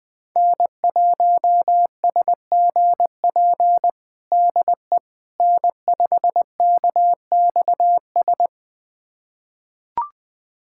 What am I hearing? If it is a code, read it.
Morse code, heard as a keyed tone, about 21 wpm, 700 Hz. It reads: N1SGP DE N5KXS
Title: USA Callsign DE Callsign - 5 Characters - 20wpm